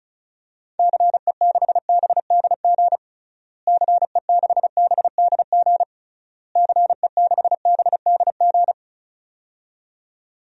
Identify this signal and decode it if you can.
Morse code, heard as a keyed tone, about 34 wpm, 700 Hz. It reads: CE6BDG CE6BDG CE6BDG
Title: DXCC-100 Call Signs - 6 Characters - Repeated 3x - 35wpm